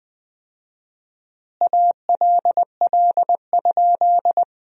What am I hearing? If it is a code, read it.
Morse code, heard as a keyed tone, about 20 wpm, 700 Hz. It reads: ALL?
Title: 4x Word-Spacing - Sentences from Top 100 Words - 20wpm